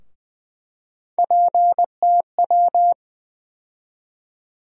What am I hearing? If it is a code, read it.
Morse code, heard as a keyed tone, about 20 wpm, 700 Hz. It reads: PTW
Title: USA Call Sign Postfixes - 3 Letters 20wpm